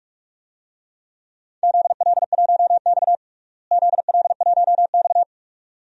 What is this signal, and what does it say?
Morse code, heard as a keyed tone, about 42 wpm, 700 Hz. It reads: ZL1X ZL1X
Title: DXCC-100 Call Signs - 4 Characters - Repeated 2x - 45wpm